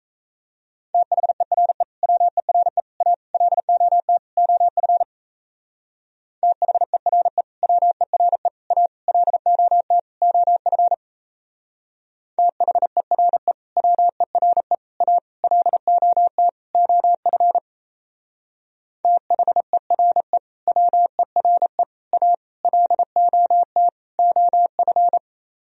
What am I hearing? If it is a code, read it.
Morse code, heard as a keyed tone, about 34 wpm, 700 Hz. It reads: THERE WERE A LOT OF THERE WERE A LOT OF THERE WERE A LOT OF THERE WERE A LOT OF
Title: Speed-Racing - Sets of 5 Words 28wpm